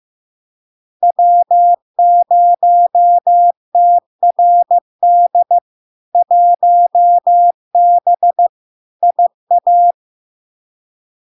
Morse code, heard as a keyed tone, about 15 wpm, 700 Hz. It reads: W0TRD 1B IA
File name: LARK-FD-2022-Log-in-CW-15wpm-700-Hz